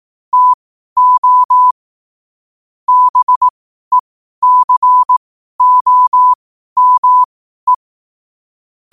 Morse code, heard as a keyed tone, about 18 wpm, spaced as for 12 wpm, 1000 Hz. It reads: TO BECOME
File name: qotdfiles / 12wpm